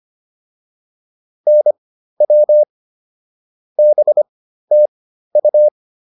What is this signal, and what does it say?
Morse code, heard as a keyed tone, about 25 wpm, spaced as for 13 wpm, 600 Hz. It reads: NW BTU
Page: intermediate-practice-files